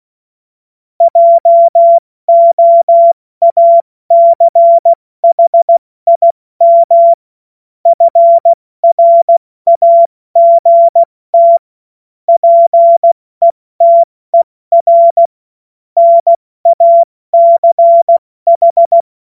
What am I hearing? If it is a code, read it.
Morse code, heard as a keyed tone, about 16 wpm, 680 Hz. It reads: JOACHIM FRAGT PETER NACH
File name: Morsekurs / Witze / Witz 10. Januar 2023 70 F80